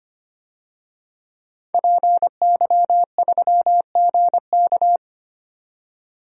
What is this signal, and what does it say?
Morse code, heard as a keyed tone, about 25 wpm, 690 Hz. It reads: PY3GK
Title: DXCC-100 Call Signs - 5 Characters - 25wpm